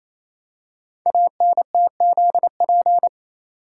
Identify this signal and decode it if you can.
Morse code, heard as a keyed tone, about 27 wpm, 700 Hz. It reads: ANTZP
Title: Code Groups - 5 Letters - 28wpm